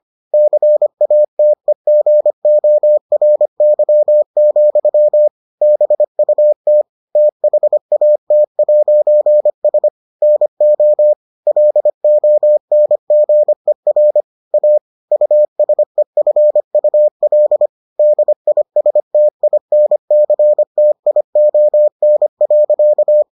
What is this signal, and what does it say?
Morse code, heard as a keyed tone, about 25 wpm, 600 Hz. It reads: CATEGORY, BUT THAT'S NO LONGER A USEFUL DISTINCTION.